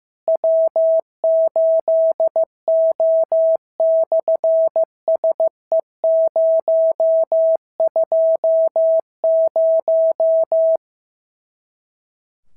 Morse code, heard as a keyed tone, about 15 wpm, 650 Hz. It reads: W8O/SE020
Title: SOTA QSO Practice 15wpm